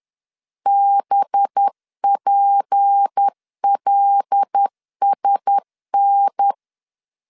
Morse code, heard as a keyed tone, about 10 wpm, 800 Hz. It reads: BPLSN
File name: LEZIONI / 8 YJ45 / 8 Y 10wpm 2615